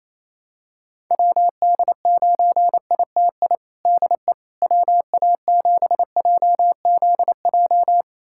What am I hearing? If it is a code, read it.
Morse code, heard as a keyed tone, about 28 wpm, 700 Hz. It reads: WD9ITI DE WA7JZJ
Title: USA Callsign DE Callsign - 6 Characters - 28wpm